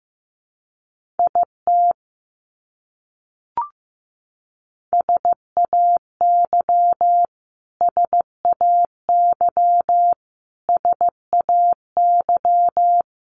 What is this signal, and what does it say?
Morse code, heard as a keyed tone, about 15 wpm, 700 Hz. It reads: IT SAY SAY SAY
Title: Top 300 Words Repeated 15wpm